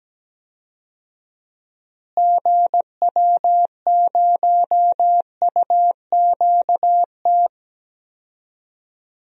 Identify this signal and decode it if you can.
Morse code, heard as a keyed tone, about 17 wpm, 700 Hz. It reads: GW0UQT
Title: DXCC-100 Call Signs - 6 Characters - Repeated 2x - 17wpm